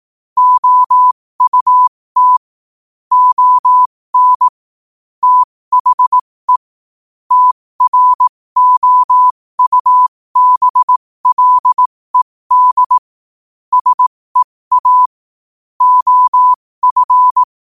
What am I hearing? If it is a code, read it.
Morse code, heard as a keyed tone, about 18 wpm, spaced as for 15 wpm, 1000 Hz. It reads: OUT ON THE TROUBLED SEA OF
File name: qotdfiles / 15wpm